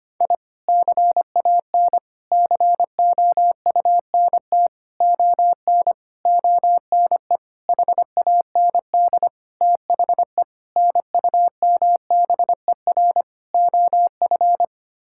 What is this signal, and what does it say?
Morse code, heard as a keyed tone, about 25 wpm, 700 Hz. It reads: I CAN COUNT ON ONE HAND THE NUMBER OF
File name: Sets of 9 Words - 25wpm